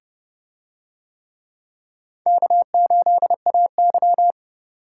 Morse code, heard as a keyed tone, about 29 wpm, 700 Hz. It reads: K8AY